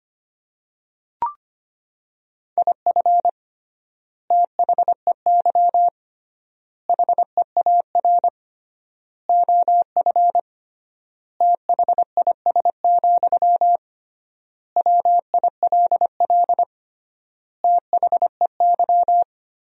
Morse code, heard as a keyed tone, about 25 wpm, 700 Hz. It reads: IF THEY HEAR OF THIS, WILL THEY
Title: 3x Word-Spacing - Sentences from Top 400 Words - 25wpm